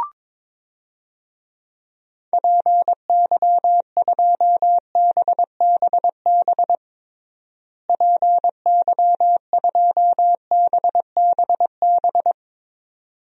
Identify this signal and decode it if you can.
Morse code, heard as a keyed tone, about 22 wpm, 700 Hz. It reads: PY2BBB PY2BBB
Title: DXCC-100 Call Signs - 6 Characters - Repeated 2x - 22wpm